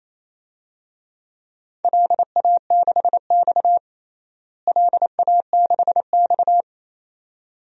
Morse code, heard as a keyed tone, about 28 wpm, 700 Hz. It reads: LA6X LA6X
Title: DXCC-100 Call Signs - 4 Characters - Repeated 2x - 28wpm